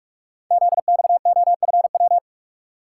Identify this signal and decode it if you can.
Morse code, heard as a keyed tone, about 44 wpm, 700 Hz. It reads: ZXOFW
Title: Code Groups - 5 Letter:Numbers - 45wpm